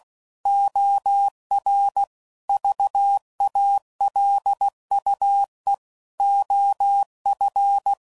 Morse code, heard as a keyed tone, about 16 wpm, 800 Hz. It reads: OR VALUE OF
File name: CW Code Groups @15WPM / Technical Subjects / XC